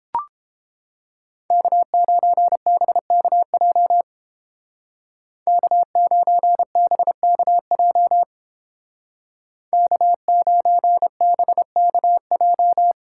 Morse code, heard as a keyed tone, about 28 wpm, 700 Hz. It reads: K9BKJ K9BKJ K9BKJ